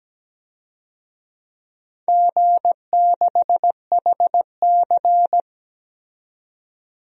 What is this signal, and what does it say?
Morse code, heard as a keyed tone, about 17 wpm, 700 Hz. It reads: G6HC
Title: DXCC-100 Call Signs - 4 Characters - 17wpm